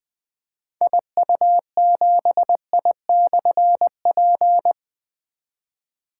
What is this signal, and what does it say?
Morse code, heard as a keyed tone, about 20 wpm, 700 Hz. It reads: IU7I/P
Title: Difficult Call Signs 20wpm